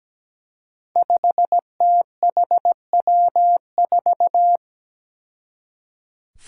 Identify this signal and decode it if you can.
Morse code, heard as a keyed tone, about 17 wpm, 700 Hz. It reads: 5THW4
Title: Five Challenging Letter-Number Combinations 17wpm